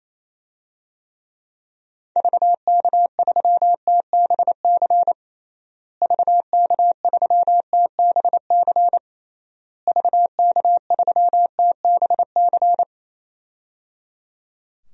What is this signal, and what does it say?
Morse code, heard as a keyed tone, about 28 wpm, 700 Hz. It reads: VK3TBC VK3TBC VK3TBC
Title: DXCC-100 Call Signs - 6 Characters - Repeated 3x - 28wpm